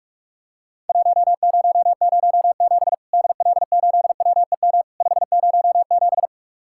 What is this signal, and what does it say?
Morse code, heard as a keyed tone, about 44 wpm, 700 Hz. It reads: 1007 DL8WEM 507
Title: Worked All Europe DX Contest - QTC Exchange - 45wpm